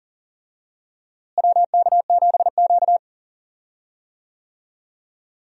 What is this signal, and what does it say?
Morse code, heard as a keyed tone, about 40 wpm, 700 Hz. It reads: WK7Q
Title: USA Call Signs - 4 Characters - Repeated 3x - 40wpm